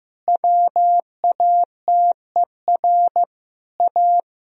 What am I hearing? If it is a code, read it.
Morse code, heard as a keyed tone, about 15 wpm, 700 Hz. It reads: WATER A
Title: Sentences from Top 1000 Words - Review Part 1 - 15wpm